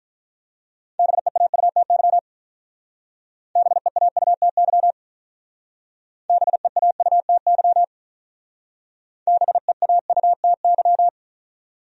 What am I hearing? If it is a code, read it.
Morse code, heard as a keyed tone, about 44 wpm, 700 Hz. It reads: BEAUTY BEAUTY BEAUTY BEAUTY